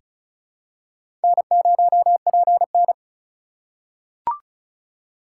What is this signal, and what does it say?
Morse code, heard as a keyed tone, about 34 wpm, 700 Hz. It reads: N0PN
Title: USA Call Signs - 4 Characters - 2x Character Spacing - 35wpm